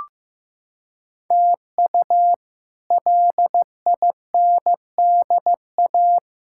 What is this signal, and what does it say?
Morse code, heard as a keyed tone, about 15 wpm, 700 Hz. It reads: TU LINDA
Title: K1USN SST QSO Practice - 15wpm